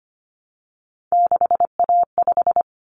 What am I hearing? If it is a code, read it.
Morse code, heard as a keyed tone, about 25 wpm, 700 Hz. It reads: 6A5
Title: Common Cut Numbers - 3-Digit 25wpm